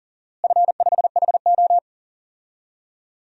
Morse code, heard as a keyed tone, about 38 wpm, 700 Hz. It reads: F5HO
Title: DXCC-100 Call Signs - 4 Characters - Repeated 2x - 40wpm